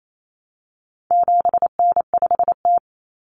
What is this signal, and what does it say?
Morse code, heard as a keyed tone, about 28 wpm, 700 Hz. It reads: 7N5T
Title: Common Cut Numbers - 4-Digit 28wpm